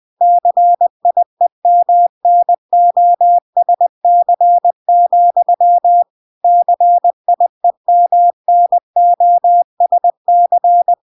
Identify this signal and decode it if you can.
Morse code, heard as a keyed tone, about 20 wpm, 700 Hz. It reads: CIEMNOSC, CIEMNOSC